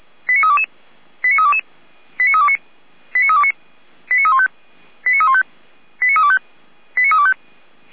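Five-tone selective call (ZVEI), heard as quick runs of five tones, tones of about 70 ms. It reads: 89322 89322 89329 89329 89315 89315 89325 89325
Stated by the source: ZVEI five-tone call 89320 received on SDR.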